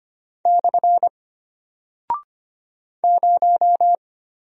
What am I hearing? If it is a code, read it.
Morse code, heard as a keyed tone, about 25 wpm, 700 Hz. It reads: / 0
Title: Single Character (Letter:Number:Punctuation:Prosign) Alt 25wpm